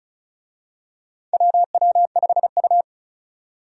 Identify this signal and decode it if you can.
Morse code, heard as a keyed tone, about 36 wpm, 700 Hz. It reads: WW5U
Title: USA Call Signs - 4 Characters - 35wpm